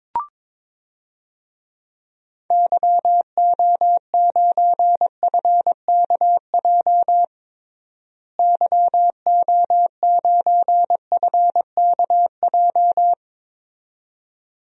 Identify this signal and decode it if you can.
Morse code, heard as a keyed tone, about 22 wpm, 700 Hz. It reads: YO9FKJ YO9FKJ
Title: DXCC-100 Call Signs - 6 Characters - Repeated 2x - 22wpm